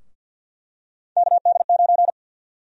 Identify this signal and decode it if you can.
Morse code, heard as a keyed tone, about 47 wpm, 700 Hz. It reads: KD9
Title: USA Call Sign Prefixes - Repeated 2x - 50wpm